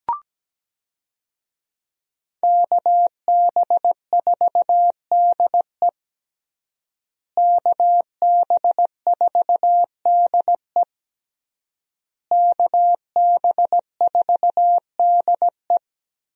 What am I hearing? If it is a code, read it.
Morse code, heard as a keyed tone, about 17 wpm, 700 Hz. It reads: KB4DE KB4DE KB4DE
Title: USA Call Signs - 5 Characters - Repeated 3x - 17wpm